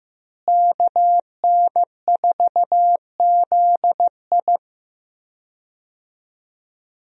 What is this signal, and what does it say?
Morse code, heard as a keyed tone, about 16 wpm, 700 Hz. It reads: KN4ZI
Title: USA Call Signs - 5 Characters - Repeated 2x - 15wpm